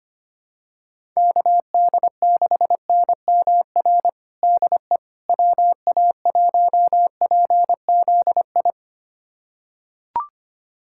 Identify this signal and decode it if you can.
Morse code, heard as a keyed tone, about 25 wpm, 700 Hz. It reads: KD6NMR DE WA1PZI
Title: USA Callsign DE Callsign 25wpm